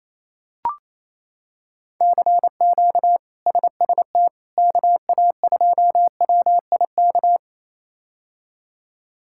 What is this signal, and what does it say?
Morse code, heard as a keyed tone, about 28 wpm, 700 Hz. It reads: CQ SST KA2WIK
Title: K1USN SST QSO Practice - 28wpm